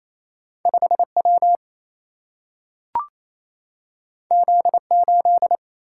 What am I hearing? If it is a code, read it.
Morse code, heard as a keyed tone, about 28 wpm, 700 Hz. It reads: 5W Z8
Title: Single-Letter Single-Number 28wpm